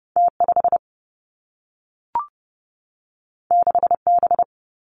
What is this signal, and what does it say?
Morse code, heard as a keyed tone, about 30 wpm, 700 Hz. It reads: T 5 6 B
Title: Two Challenging Letter-Number Combinations 30wpm